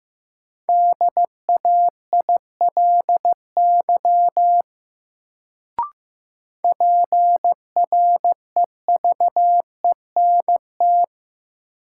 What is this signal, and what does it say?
Morse code, heard as a keyed tone, about 15 wpm, 700 Hz. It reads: DAILY PREVENT
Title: Sentences from Top 1100 Words - New Word Drill Alt - 15wpm